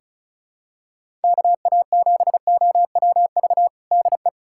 Morse code, heard as a keyed tone, about 34 wpm, 700 Hz. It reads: KA7OWV DE
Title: USA Callsign DE Callsign - 6 Characters - 35wpm